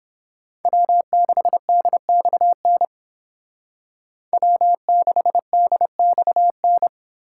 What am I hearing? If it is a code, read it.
Morse code, heard as a keyed tone, about 28 wpm, 700 Hz. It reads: W6DXN W6DXN